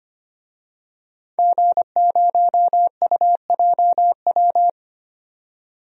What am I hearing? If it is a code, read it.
Morse code, heard as a keyed tone, about 25 wpm, 700 Hz. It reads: G0UJW
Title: DXCC-100 Call Signs - 5 Characters - 25wpm